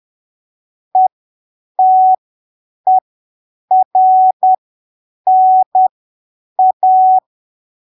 Morse code, heard as a keyed tone, about 10 wpm, spaced as for 7 wpm, 750 Hz. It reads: ETERNA
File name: arquivos / texto / texto 27 10ppm